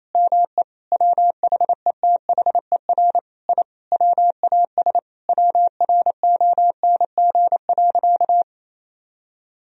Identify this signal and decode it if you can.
Morse code, heard as a keyed tone, about 27 wpm, 700 Hz. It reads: ME WHETHER I WAS WRONG.
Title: Sentences from Top 1000 Words - Review Part 3 - 28wpm